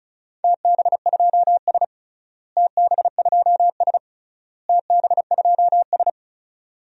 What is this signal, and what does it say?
Morse code, heard as a keyed tone, about 34 wpm, 700 Hz. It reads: TB2S TB2S TB2S
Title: DXCC-100 Call Signs - 4 Characters - Repeated 3x - 35wpm